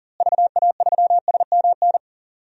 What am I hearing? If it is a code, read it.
Morse code, heard as a keyed tone, about 40 wpm, 700 Hz. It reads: VA3SMN
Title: DXCC-100 Call Signs - 6 Characters - Repeated 2x - 40wpm